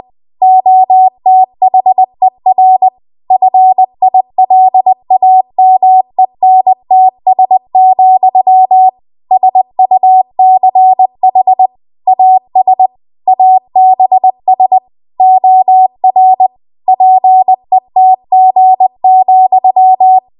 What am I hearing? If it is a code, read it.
Morse code, heard as a keyed tone, about 20 wpm, 750 Hz. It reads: OTHER FILAMENTS, SUCH AS ABS OR PETG,
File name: Archive / 20 WPM / 20WPM